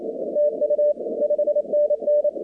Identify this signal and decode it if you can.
Morse code, heard as a keyed tone, about 28 wpm, 590 Hz, with radio noise. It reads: TU 5NN
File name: contests / 2019-cw-wpx-kt5j-k5pi- -2330 0-24